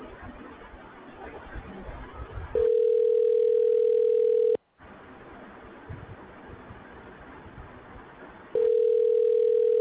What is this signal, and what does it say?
Ringback tone (North America), 440 + 480 Hz.